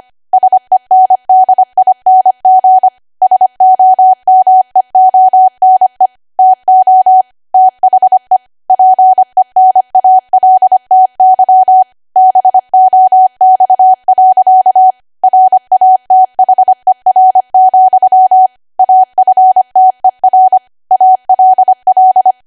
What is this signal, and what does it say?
Morse code, heard as a keyed tone, about 25 wpm, 750 Hz. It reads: SENDING SOMEONE TO THE PENALTY BOX. RATHER, AFTER ALL
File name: Archive / 25 WPM / 25WPM